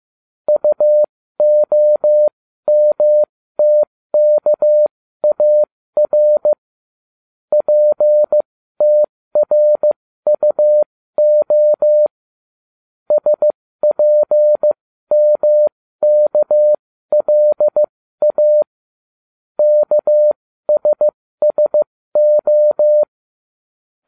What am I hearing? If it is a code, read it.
Morse code, heard as a keyed tone, about 15 wpm, spaced as for 13 wpm, 600 Hz. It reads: UOMTKAR PTRUO SPMKLA KSSO